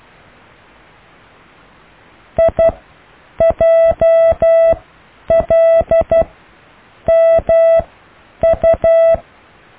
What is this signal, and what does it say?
Morse code, heard as a keyed tone, about 12 wpm, spaced as for 8 wpm, 660 Hz. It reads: IJLMU
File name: cw / IJLUM+CDRO+EKTPV+BFHNW